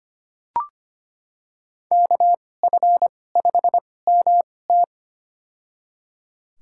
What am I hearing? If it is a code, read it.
Morse code, heard as a keyed tone, about 25 wpm, spaced as for 18 wpm, 700 Hz. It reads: KF5MT
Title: USA Call Signs - 5 Characters - 2x Character Spacing - 25wpm